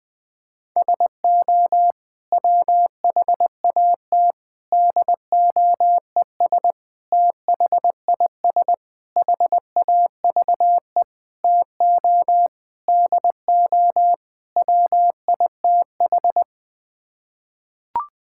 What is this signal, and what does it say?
Morse code, heard as a keyed tone, about 20 wpm, 700 Hz. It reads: SO WHAT DOES THIS HAVE TO DO WITH
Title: Sets of 8 Words - 20wpm